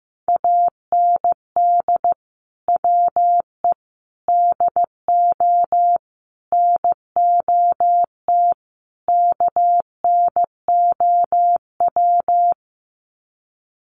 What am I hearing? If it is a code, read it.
Morse code, heard as a keyed tone, about 15 wpm, 700 Hz. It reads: AND WE DO NOT KNOW
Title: Sets of 5 Words Encore 15wpm